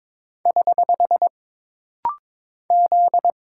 Morse code, heard as a keyed tone, about 22 wpm, 700 Hz. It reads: <HH> Z
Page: This rapid-fire timing is geared to push your ICR (Instant Character Recognition) skills.